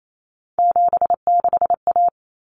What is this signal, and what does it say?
Morse code, heard as a keyed tone, about 28 wpm, 700 Hz. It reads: 76A
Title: Common Cut Numbers - 3-Digit 28wpm